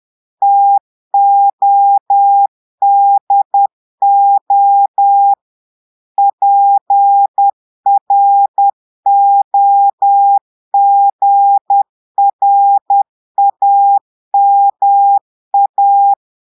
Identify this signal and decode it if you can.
Morse code, heard as a keyed tone, about 10 wpm, 800 Hz. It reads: TODO PROGRAMA
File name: portugais texte en clair 2 vitesse 10 mots minutes